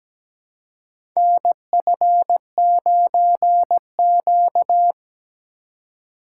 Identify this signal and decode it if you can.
Morse code, heard as a keyed tone, about 17 wpm, 700 Hz. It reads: NF9Q
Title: USA Call Signs - 4 Characters - 4x Character Spacing - 17wpm